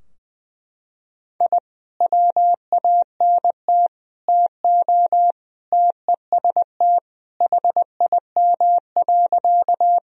Morse code, heard as a keyed tone, about 19 wpm, 700 Hz. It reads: I WANT TO TEST HIM.
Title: Sentences from Top 1000 Words - Review Part 3 - 20wpm